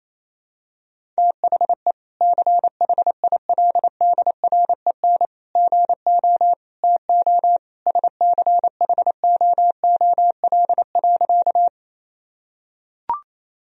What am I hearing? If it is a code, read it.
Morse code, heard as a keyed tone, about 28 wpm, 700 Hz. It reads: THE CHILDREN GO TO SCHOOL.
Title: Sentences from Top 1000 Words - Review Part 1 - 28wpm